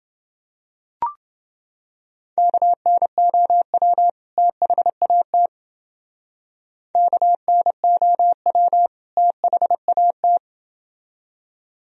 Morse code, heard as a keyed tone, about 28 wpm, 700 Hz. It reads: KNOW THAT KNOW THAT